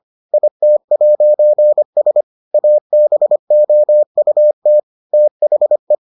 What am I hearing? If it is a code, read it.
Morse code, heard as a keyed tone, about 25 wpm, 600 Hz. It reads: IT'S ABOUT THE